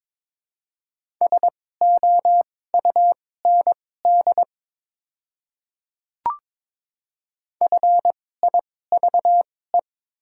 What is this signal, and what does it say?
Morse code, heard as a keyed tone, about 23 wpm, spaced as for 16 wpm, 700 Hz. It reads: SOUND FIVE
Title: 2x Character Spacing - Top 500 Words - 22wpm